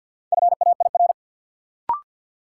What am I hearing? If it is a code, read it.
Morse code, heard as a keyed tone, about 52 wpm, 700 Hz. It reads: FAIR
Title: Sentences from Top 1300 Words - New Word Drill Alt - 50wpm